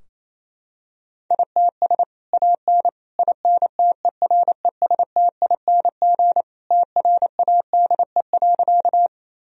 Morse code, heard as a keyed tone, about 28 wpm, 700 Hz. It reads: ITS AN INTERESTING TRADE.
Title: Sentences from Top 1000 Words - Review Part 4 - 28wpm